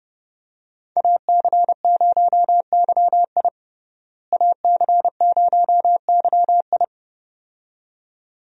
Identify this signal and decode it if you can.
Morse code, heard as a keyed tone, about 32 wpm, 700 Hz. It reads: AC0YI AC0YI
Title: USA Call Signs - 5 Characters - Repeated 2x - 30wpm